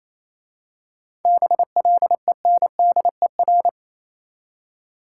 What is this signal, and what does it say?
Morse code, heard as a keyed tone, about 27 wpm, 690 Hz. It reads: BLENDER
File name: Easy Words - Speed-Racing with Hints - 22wpm